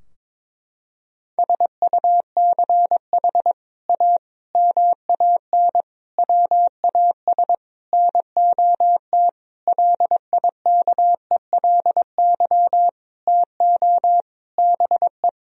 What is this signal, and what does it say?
Morse code, heard as a keyed tone, about 21 wpm, 700 Hz. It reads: SUCH A MAN WAS NOT LIKELY TO BE
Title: Sentences from Top 1000 Words - Review Part 4 - 22wpm